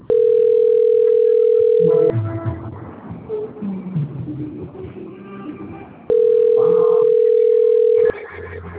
Ringback tone (North America), 440 + 480 Hz.